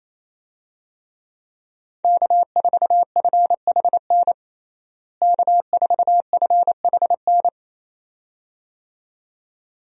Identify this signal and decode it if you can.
Morse code, heard as a keyed tone, about 28 wpm, 700 Hz. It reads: K4FHN K4FHN
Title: USA Call Signs - 5 Characters - Repeated 2x - 28wpm